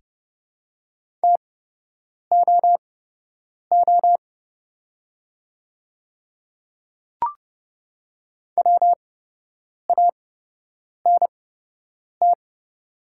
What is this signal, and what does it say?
Morse code, heard as a keyed tone, about 30 wpm, spaced as for 8 wpm, 700 Hz. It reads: TOO WANT
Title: 8x Character Spacing - Top 300 Words - 30wpm